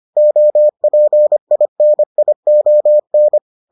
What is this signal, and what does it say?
Morse code, heard as a keyed tone, about 25 wpm, 600 Hz. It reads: OPINION